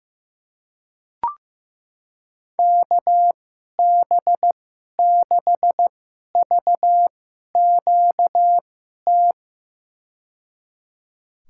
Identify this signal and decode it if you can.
Morse code, heard as a keyed tone, about 15 wpm, spaced as for 11 wpm, 700 Hz. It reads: KB6VQT
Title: USA Call Signs - 6 Characters - 2x Character Spacing - 15wpm